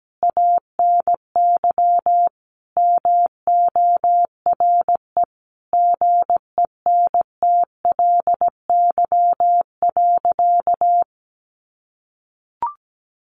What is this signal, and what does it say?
Morse code, heard as a keyed tone, about 17 wpm, 700 Hz. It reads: ANY MORE GENTLY.
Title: Sentences from Top 1500 Words 17wpm